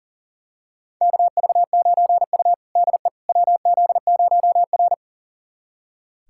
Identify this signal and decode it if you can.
Morse code, heard as a keyed tone, about 40 wpm, 700 Hz. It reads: KV9U DE WZ0R
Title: USA Callsign DE Callsign - 4 Characters - 40wpm